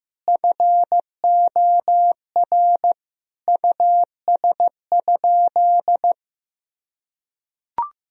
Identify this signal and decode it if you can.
Morse code, heard as a keyed tone, about 16 wpm, 700 Hz. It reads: FOR US?
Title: Sentences from Top 1000 Words - Review Part 3 - 15wpm